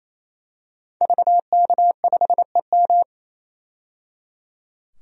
Morse code, heard as a keyed tone, about 28 wpm, 700 Hz. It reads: VK5EM
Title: DXCC-100 Call Signs - 5 Characters - 28wpm